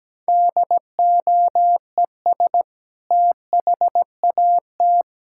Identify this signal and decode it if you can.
Morse code, heard as a keyed tone, about 17 wpm, 700 Hz. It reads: DOES THAT
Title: Sentences from Top 1600 Words 17wpm